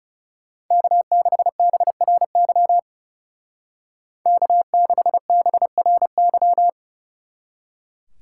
Morse code, heard as a keyed tone, about 34 wpm, 700 Hz. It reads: K6BRY K6BRY
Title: Speed Racing - USA 1x3 Call Signs 30wpm